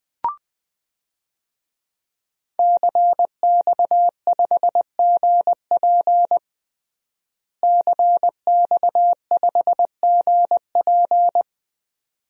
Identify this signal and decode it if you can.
Morse code, heard as a keyed tone, about 20 wpm, 700 Hz. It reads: CX5GP CX5GP
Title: DXCC-100 Call Signs - 5 Characters - Repeated 2x - 20wpm